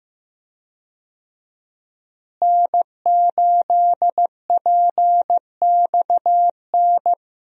Morse code, heard as a keyed tone, about 15 wpm, 700 Hz. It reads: N8PXN
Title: USA Call Signs - 5 Characters - Repeated 3x - 15wpm